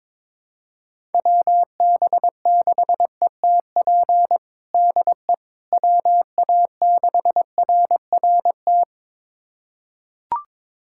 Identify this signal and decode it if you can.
Morse code, heard as a keyed tone, about 22 wpm, 700 Hz. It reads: WB6ETP DE WA6RRT
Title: USA Callsign DE Callsign - 6 Characters - 22wpm